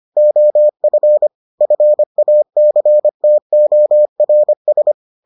Morse code, heard as a keyed tone, about 25 wpm, 600 Hz. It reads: OF FACTORS